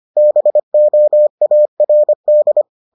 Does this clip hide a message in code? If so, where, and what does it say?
Yes: Morse code from 0.2 to 2.6 s, heard as a keyed tone, about 25 wpm, 600 Hz. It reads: BOARD